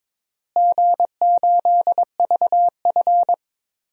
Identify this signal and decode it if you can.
Morse code, heard as a keyed tone, about 22 wpm, 700 Hz. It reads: G8VF
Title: DXCC-100 Call Signs - 4 Characters - Repeated 2x - 22wpm